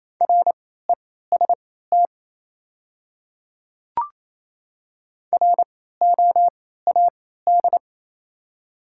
Morse code, heard as a keyed tone, about 28 wpm, spaced as for 16 wpm, 700 Hz. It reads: REST ROAD
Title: 3x Character Spacing - Top 500 Words - 28wpm